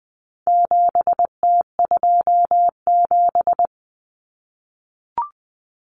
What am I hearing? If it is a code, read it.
Morse code, heard as a keyed tone, about 20 wpm, 700 Hz. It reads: 7T27
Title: Common Cut Numbers - 4-Digit 20wpm